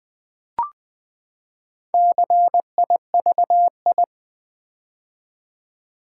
Morse code, heard as a keyed tone, about 20 wpm, 700 Hz. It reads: CIVI
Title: Build A Word - 4 Letters At A Time - 12-Letter Words - 20wpm